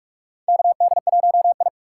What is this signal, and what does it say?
Morse code, heard as a keyed tone, about 46 wpm, 700 Hz. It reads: KD1I
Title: USA Call Signs - 3x Character Spacing - 45wpm